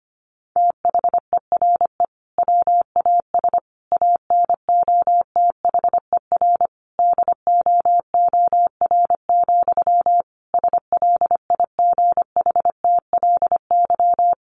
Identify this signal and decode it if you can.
Morse code, heard as a keyed tone, about 25 wpm, 700 Hz. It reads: THERE WAS ANOTHER DOOR, SLIGHTLY
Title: Sentences from Top 1300 Words 25wpm